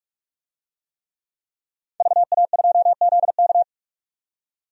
Morse code, heard as a keyed tone, about 45 wpm, 700 Hz. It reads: VA2ZK
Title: DXCC-100 Call Signs - 5 Characters - 45wpm